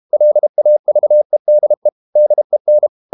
Morse code, heard as a keyed tone, about 32 wpm, 600 Hz. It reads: LAVEDE DEN
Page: Ferie 32 wpm